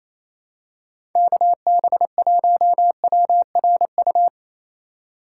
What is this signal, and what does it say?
Morse code, heard as a keyed tone, about 28 wpm, 700 Hz. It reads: KB1WRU
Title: USA Call Signs - 6 Characters - 2x Character Spacing - 28wpm